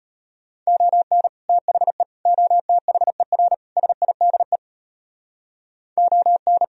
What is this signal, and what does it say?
Morse code, heard as a keyed tone, about 38 wpm, 700 Hz. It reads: ON THE OTHER SIDE ON